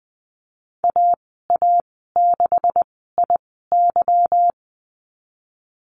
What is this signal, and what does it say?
Morse code, heard as a keyed tone, about 20 wpm, spaced as for 15 wpm, 700 Hz. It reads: AA6IY
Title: 186 Lesson 30g - Letter b - Call Signs 20wpm